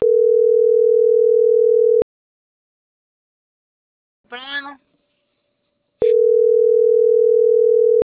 Ringback tone (North America), 440 + 480 Hz.